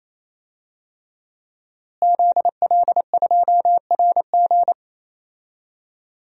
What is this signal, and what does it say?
Morse code, heard as a keyed tone, about 27 wpm, 700 Hz. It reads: ZL2RG
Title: DXCC-100 Call Signs - 5 Characters - Repeated 3x - 28wpm